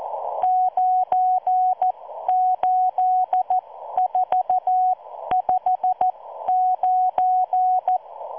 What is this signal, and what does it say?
Morse code, heard as a keyed tone, about 14 wpm, spaced as for 12 wpm, 740 Hz. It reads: 98459